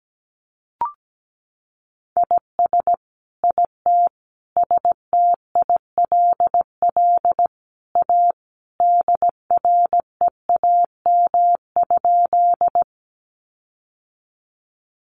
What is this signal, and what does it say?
Morse code, heard as a keyed tone, about 17 wpm, 700 Hz. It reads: IS IT STILL A DREAM?
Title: Sentences from Top 1500 Words 17wpm